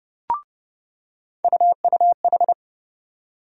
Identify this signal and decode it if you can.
Morse code, heard as a keyed tone, about 30 wpm, 700 Hz. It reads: UUH
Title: USA Call Sign Postfixes - 3 Letters 30wpm